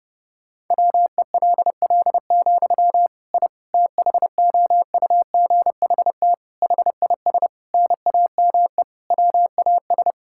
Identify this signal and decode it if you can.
Morse code, heard as a keyed tone, about 30 wpm, 700 Hz. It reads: WELL, I THOUGHT HIS NAME WAS
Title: Sentences from Top 1000 Words - Review Part 5 - 30wpm